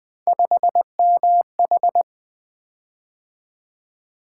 Morse code, heard as a keyed tone, about 20 wpm, 700 Hz. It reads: 5MH
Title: Three Letter-Number Combinations - 20wpm